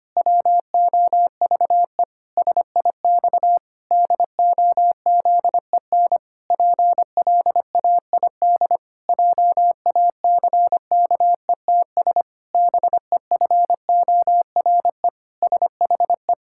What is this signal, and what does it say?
Morse code, heard as a keyed tone, about 25 wpm, 700 Hz. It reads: WOVE SIX DOZEN PLAID JACKETS BEFORE SHE
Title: Pangram Challenge v2 - 25wpm